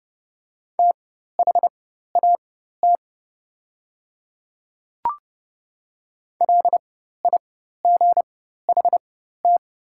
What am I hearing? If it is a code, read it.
Morse code, heard as a keyed tone, about 30 wpm, spaced as for 14 wpm, 700 Hz. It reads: THAT LIGHT
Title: 4x Character Spacing - Top 300 Words - 30wpm